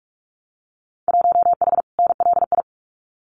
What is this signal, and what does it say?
Morse code, heard as a keyed tone, about 46 wpm, 700 Hz. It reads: 1H NLI
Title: Winter Field Day Exchanges 45wpm